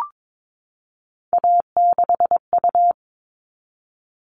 Morse code, heard as a keyed tone, about 22 wpm, 700 Hz. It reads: A6U
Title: All Cut Numbers - 3-Digit 22wpm